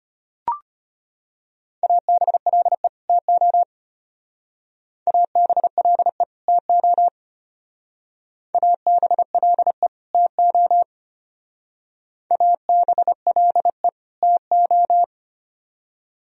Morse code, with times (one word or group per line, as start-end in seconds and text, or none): 1.8-2.9 ABLE
3.1-3.6 TO
5.1-6.2 ABLE
6.5-7.1 TO
8.5-9.9 ABLE
10.1-10.8 TO
12.3-13.9 ABLE
14.2-15.0 TO